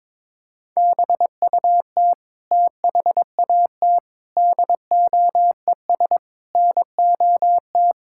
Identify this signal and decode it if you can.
Morse code, heard as a keyed tone, about 22 wpm, 700 Hz. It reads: BUT THAT DOES NOT
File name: Sentences with 7-Word Phrases - 22wpm